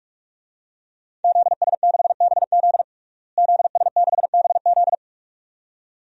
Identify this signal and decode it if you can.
Morse code, heard as a keyed tone, about 41 wpm, 700 Hz. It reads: ZS6BZ ZS6BZ
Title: DXCC-100 Call Signs - 5 Characters - Repeated 2x - 45wpm